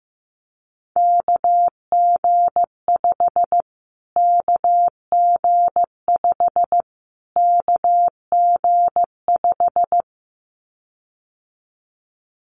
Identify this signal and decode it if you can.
Morse code, heard as a keyed tone, about 15 wpm, 700 Hz. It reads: KG5 KG5 KG5
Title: Callsign Prefixes for Top 25 DXCC - Repeated 3x - 15wpm